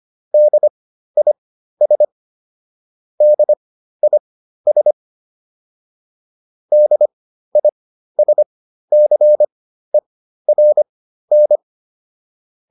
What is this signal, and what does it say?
Morse code, heard as a keyed tone, about 25 wpm, spaced as for 13 wpm, 600 Hz. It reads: DIS DIS DISCERN
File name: DIS7 13